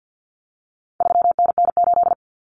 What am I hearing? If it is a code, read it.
Morse code, heard as a keyed tone, about 49 wpm, 700 Hz. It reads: 3NN8
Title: Common Cut Numbers - 4-Digit 50wpm